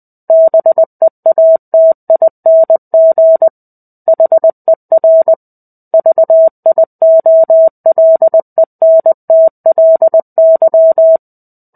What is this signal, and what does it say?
Morse code, heard as a keyed tone, about 20 wpm, 650 Hz. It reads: BEATING HER VIOLENTLY
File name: 05-Chapter-05 20 WPM